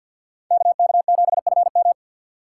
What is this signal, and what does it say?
Morse code, heard as a keyed tone, about 53 wpm, 700 Hz. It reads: KK7FM
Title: USA Call Signs - 4x Character Spacing - 50wpm